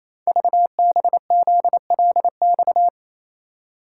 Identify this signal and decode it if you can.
Morse code, heard as a keyed tone, about 28 wpm, 700 Hz. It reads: VBZLX
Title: Code Groups - 5 Letters - 28wpm